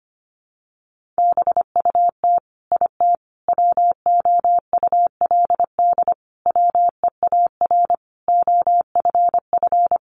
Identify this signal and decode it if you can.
Morse code, heard as a keyed tone, about 25 wpm, 700 Hz. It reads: BUT IT WOULD WEAR OFF
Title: Sentences from Top 1500 Words 25wpm